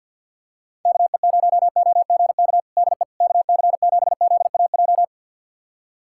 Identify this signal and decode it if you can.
Morse code, heard as a keyed tone, about 53 wpm, 700 Hz. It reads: KE0OGK DE KC7ZAJ
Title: USA Callsign DE Callsign - 6 Characters - 50wpm